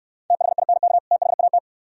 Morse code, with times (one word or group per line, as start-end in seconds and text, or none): 0.3-1.0 THINK
1.1-1.6 THAT